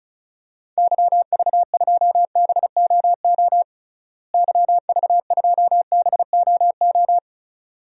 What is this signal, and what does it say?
Morse code, heard as a keyed tone, about 36 wpm, 700 Hz. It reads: YV2BOO YV2BOO
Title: DXCC-100 Call Signs - 6 Characters - Repeated 2x - 35wpm